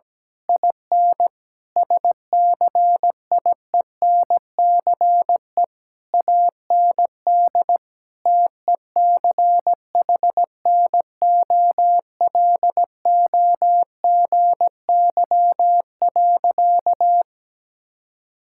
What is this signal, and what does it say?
Morse code, heard as a keyed tone, about 17 wpm, 700 Hz. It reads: IN SCIENCE AND TECHNOLOGY.
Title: Sentences from Top 1000 Words - Review Part 6 - 17wpm